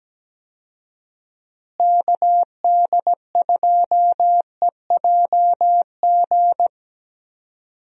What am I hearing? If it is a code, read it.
Morse code, heard as a keyed tone, about 17 wpm, 700 Hz. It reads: KD2EJG
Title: USA Call Signs - 6 Characters - Repeated 3x - 17wpm